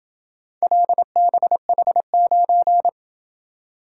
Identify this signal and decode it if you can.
Morse code, heard as a keyed tone, about 27 wpm, 700 Hz. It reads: LBH9